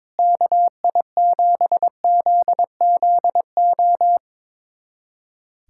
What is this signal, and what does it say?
Morse code, heard as a keyed tone, about 22 wpm, 700 Hz. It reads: KI7ZZO
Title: USA Callsign DE Callsign - 6 Characters - 22wpm